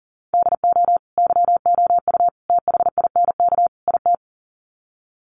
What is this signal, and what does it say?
Morse code, heard as a keyed tone, about 40 wpm, 700 Hz. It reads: DO YOU THINK IT
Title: Sets of 4 Words 40wpm